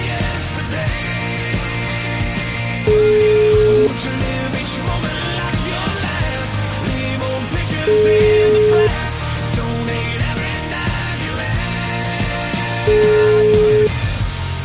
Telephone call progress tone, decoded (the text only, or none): ringback tone